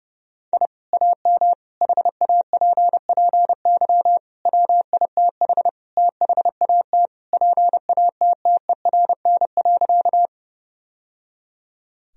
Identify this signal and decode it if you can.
Morse code, heard as a keyed tone, about 31 wpm, 700 Hz. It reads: I AM HAPPY WITH THAT PATTERN.
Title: Sentences from Top 1000 Words - Review Part 2 - 30wpm